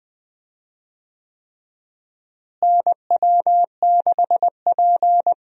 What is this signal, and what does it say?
Morse code, heard as a keyed tone, about 21 wpm, 700 Hz. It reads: NW6P
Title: USA Call Signs - 4 Characters - Repeated 3x - 20wpm